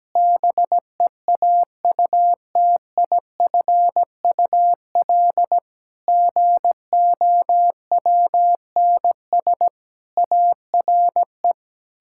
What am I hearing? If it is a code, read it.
Morse code, heard as a keyed tone, about 17 wpm, 700 Hz. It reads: BEAUTIFUL GOWNS ARE
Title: Pangram Challenge v2 - 17wpm